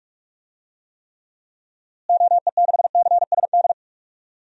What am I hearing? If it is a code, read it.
Morse code, heard as a keyed tone, about 41 wpm, 700 Hz. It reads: OE6CSD
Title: DXCC-100 Call Signs - 6 Characters - Repeated 2x - 45wpm